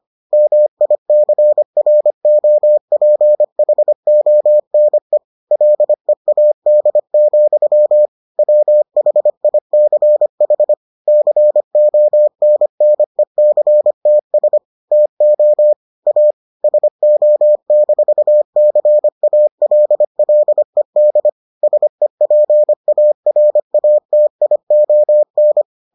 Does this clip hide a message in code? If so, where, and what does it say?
Yes: Morse code from 0.3 to 25.6 s, heard as a keyed tone, about 25 wpm, 600 Hz. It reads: MICROPHONE LEAD, WHICH CONNECTS TO A SO-CALLED SEPARATION